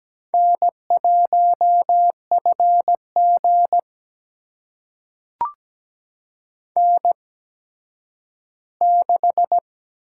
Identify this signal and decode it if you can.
Morse code, heard as a keyed tone, about 17 wpm, spaced as for 10 wpm, 700 Hz. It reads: N1FG N6
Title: USA Call Signs - 4 Characters - 8x Character Spacing - 17wpm